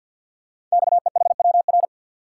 Morse code, heard as a keyed tone, about 51 wpm, 700 Hz. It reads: XEHWR
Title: Code Groups - 5 Letters - 50wpm